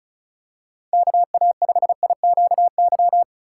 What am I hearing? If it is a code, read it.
Morse code, heard as a keyed tone, about 36 wpm, 700 Hz. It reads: KA5IQY
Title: USA Call Signs - 6 Characters - 6x Character Spacing - 35wpm